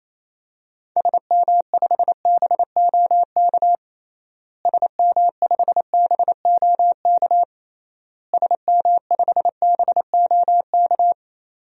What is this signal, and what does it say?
Morse code, heard as a keyed tone, about 28 wpm, 700 Hz. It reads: SM5BOK SM5BOK SM5BOK
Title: DXCC-100 Call Signs - 6 Characters - Repeated 3x - 28wpm